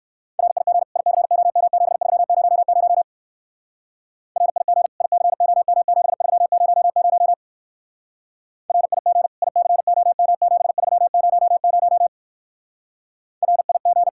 Morse code, heard as a keyed tone, about 56 wpm, 700 Hz. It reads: RIG ICOM7300 RIG ICOM7300 RIG ICOM7300 RIG